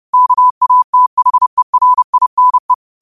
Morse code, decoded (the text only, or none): KATHERINE